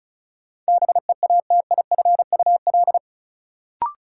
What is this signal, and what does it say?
Morse code, heard as a keyed tone, about 34 wpm, 700 Hz. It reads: BEATIFUL
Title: Sentences from Top 1300 Words - New Word Drill - 35wpm